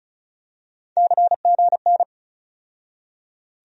Morse code, heard as a keyed tone, about 34 wpm, 700 Hz. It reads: CGN
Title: USA Call Sign Postfixes - 3 Letters 35wpm